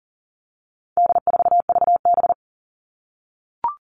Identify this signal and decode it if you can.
Morse code, heard as a keyed tone, about 40 wpm, 700 Hz. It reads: D4VB
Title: All Cut Numbers - 4-Digit 40wpm